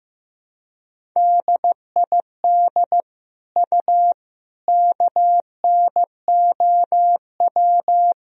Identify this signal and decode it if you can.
Morse code, heard as a keyed tone, about 15 wpm, 700 Hz. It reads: DID U KNOW
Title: Sentences from Top 100 Words and Abbreviations 15wpm